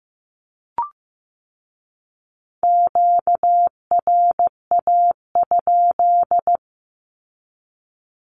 Morse code, heard as a keyed tone, about 15 wpm, 700 Hz. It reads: QRA?
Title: Q-Codes with Spoken Meaning - 15wpm